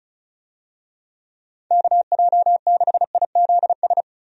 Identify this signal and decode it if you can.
Morse code, heard as a keyed tone, about 36 wpm, 700 Hz. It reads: KJ6IZS
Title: USA Call Signs - 6 Characters - 35wpm